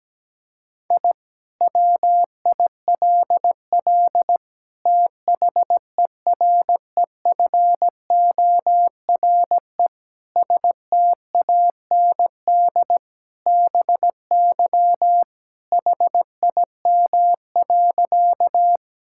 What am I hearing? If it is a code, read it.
Morse code, heard as a keyed tone, about 17 wpm, 700 Hz. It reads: I WILL THEREFORE STAND BY HIM.
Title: Sentences from Top 1000 Words - Review Part 5 - 17wpm